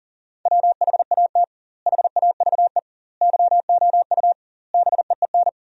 Morse code, heard as a keyed tone, about 37 wpm, 700 Hz. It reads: WHAT HAVE YOU BEEN
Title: 2x Word-Spacing - Sentences from Top 200 Words - 40wpm